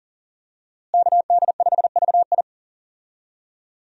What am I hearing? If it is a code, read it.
Morse code, heard as a keyed tone, about 40 wpm, 700 Hz. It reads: KD5VI
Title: USA Call Signs - 5 Characters - Repeated 3x - 40wpm